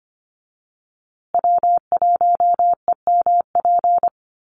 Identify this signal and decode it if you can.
Morse code, heard as a keyed tone, about 25 wpm, 700 Hz. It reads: W1EMP
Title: USA Call Signs Repeated 3x 25wpm